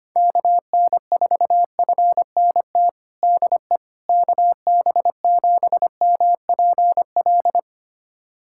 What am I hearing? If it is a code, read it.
Morse code, heard as a keyed tone, about 25 wpm, 700 Hz. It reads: KN4FNT DE KB7MPL
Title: USA Callsign DE Callsign - 6 Characters - 25wpm